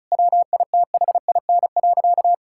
Morse code, heard as a keyed tone, about 36 wpm, 700 Hz. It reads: WITHIN.
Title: Sentences from Top 1600 Words 35wpm